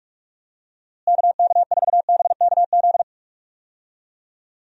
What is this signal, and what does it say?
Morse code, heard as a keyed tone, about 43 wpm, 700 Hz. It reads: KK4BKZ
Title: USA Call Signs - 6 Characters - Repeated 3x - 45wpm